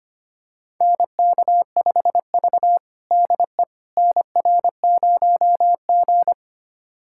Morse code, heard as a keyed tone, about 25 wpm, 700 Hz. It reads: NK5V DE NR0G
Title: USA Callsign DE Callsign - 4 Characters - 25wpm